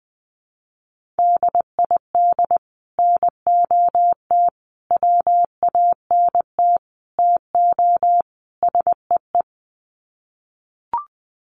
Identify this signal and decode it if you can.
Morse code, heard as a keyed tone, about 20 wpm, 700 Hz. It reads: DID NOT WANT TO SEE
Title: Sets of 5 Words Encore2 20wpm